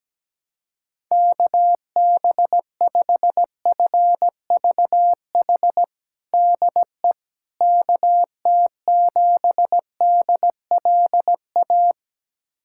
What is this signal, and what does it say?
Morse code, heard as a keyed tone, about 17 wpm, 700 Hz. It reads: KB5FVH DE KT7DLA
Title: USA Callsign DE Callsign 17wpm